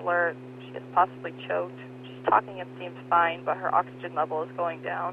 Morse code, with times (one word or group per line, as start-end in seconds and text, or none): none